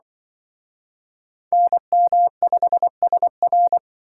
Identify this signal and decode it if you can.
Morse code, heard as a keyed tone, about 25 wpm, 700 Hz. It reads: NM5SR
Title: Speed Racing - USA Call Signs 20wpm